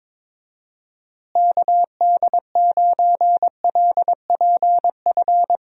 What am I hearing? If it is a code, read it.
Morse code, heard as a keyed tone, about 21 wpm, 700 Hz. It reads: KD9LPF
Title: USA Call Signs - 6 Characters - Repeated 2x - 22wpm